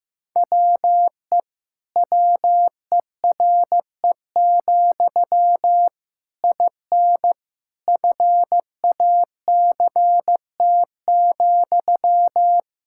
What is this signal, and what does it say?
Morse code, heard as a keyed tone, about 16 wpm, 700 Hz. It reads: WE WERE, IN FACT,
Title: Sentences from Top 1000 Words - Review Part 5 - 15wpm